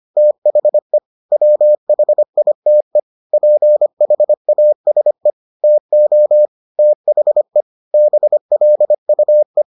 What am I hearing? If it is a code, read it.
Morse code, heard as a keyed tone, about 25 wpm, 600 Hz. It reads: THE WHITE PHASE TO THE BLUE